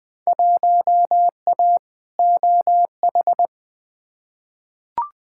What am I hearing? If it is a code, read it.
Morse code, heard as a keyed tone, about 20 wpm, 700 Hz. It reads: 1A OH
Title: Field Day Exchanges - 20wpm